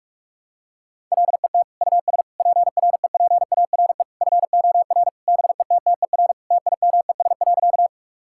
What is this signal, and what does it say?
Morse code, heard as a keyed tone, about 46 wpm, 700 Hz. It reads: LET US PREPARE FOR BETTER TIMES.
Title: Sentences from Top 1000 Words - Review Part 6 - 45wpm